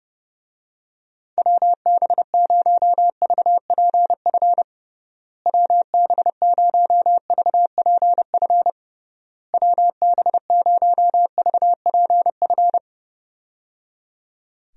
Morse code, heard as a keyed tone, about 30 wpm, 700 Hz. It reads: WB0VPF WB0VPF WB0VPF
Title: USA Call Signs - 6 Characters - Repeated 3x - 30wpm